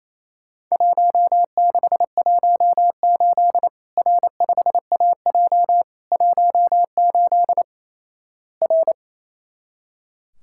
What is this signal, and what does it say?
Morse code, heard as a keyed tone, about 28 wpm, 700 Hz. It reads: 1618 R5AJ 18 R
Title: Worked All Europe DX Contest - QTC Exchange - 28wpm